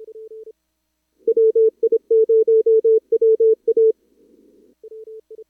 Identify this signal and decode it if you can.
Morse code, heard as a keyed tone, about 26 wpm, 440 Hz. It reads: WI0WA
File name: CWSprint / Feb2022 / k5tr-feb-na-sprint-cw-k5ot -0100